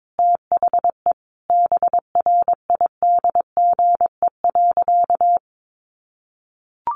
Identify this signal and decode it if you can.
Morse code, heard as a keyed tone, about 22 wpm, 700 Hz. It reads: THE BRIDGE.
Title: Sentences from Top 1400 Words 22wpm